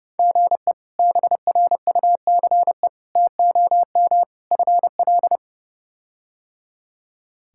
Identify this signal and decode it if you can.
Morse code, heard as a keyed tone, about 30 wpm, 700 Hz. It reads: GE BRUCE TOM FL
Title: K1USN SST QSO Practice - 30wpm